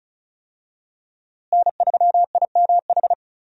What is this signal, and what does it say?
Morse code, heard as a keyed tone, about 36 wpm, 700 Hz. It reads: N3IMH
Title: USA Call Signs - 5 Characters - Repeated 2x - 35wpm